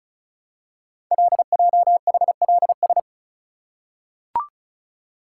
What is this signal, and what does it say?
Morse code, heard as a keyed tone, about 36 wpm, 700 Hz. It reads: LJHLS
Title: Code Groups - 5 Letters - 35wpm